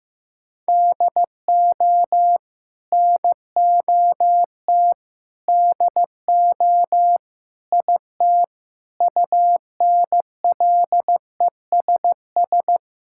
Morse code, heard as a keyed tone, about 15 wpm, 700 Hz. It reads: DO NOT DO IT UNLESS
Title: Sentences from Top 1000 Words - Review Part 6 - 15wpm